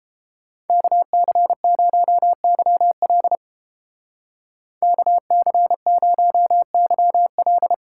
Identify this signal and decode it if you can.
Morse code, heard as a keyed tone, about 29 wpm, 700 Hz. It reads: KC0YL KC0YL